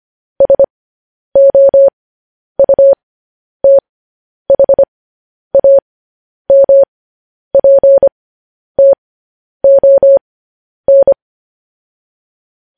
Morse code, heard as a keyed tone, about 25 wpm, spaced as for 10 wpm, 550 Hz. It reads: SOUTHAMPTON